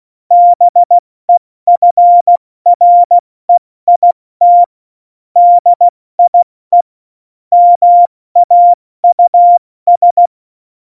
Morse code, heard as a keyed tone, about 16 wpm, 700 Hz. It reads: BEFREIT DIE MAUS
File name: Morsekurs / Witze / Witz 18. September 2018, 70 F80